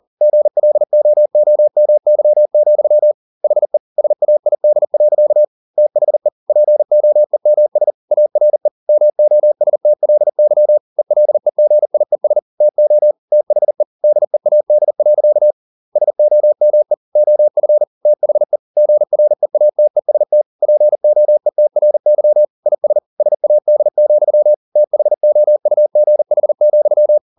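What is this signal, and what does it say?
Morse code, heard as a keyed tone, about 40 wpm, 600 Hz. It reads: GLOOMY, HE SAID. THE POEMS ARE MOSTLY ELEGIES TO THE DEAD. SOME OF THE GREATEST POETRY IS SAD, THOUGH,
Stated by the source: Morse code at 40 wpm, practice text, CW med Gnister.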